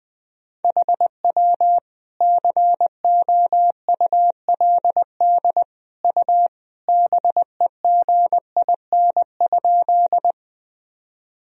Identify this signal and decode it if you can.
Morse code, heard as a keyed tone, about 20 wpm, 700 Hz. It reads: HW COULD U BEGIN?
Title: Sentences from Top 300 Words and Abbreviations 20wpm